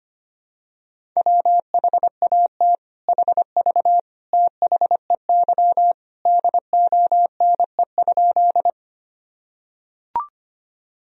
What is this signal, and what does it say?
Morse code, heard as a keyed tone, about 25 wpm, 700 Hz. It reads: WHAT HV THEY DONE?
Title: Sentences from Top 500 Words and Abbreviations 25wpm